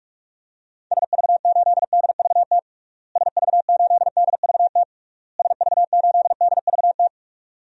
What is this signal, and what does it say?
Morse code, heard as a keyed tone, about 42 wpm, 700 Hz. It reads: SV8DVT SV8DVT SV8DVT
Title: DXCC-100 Call Signs - 6 Characters - Repeated 3x - 45wpm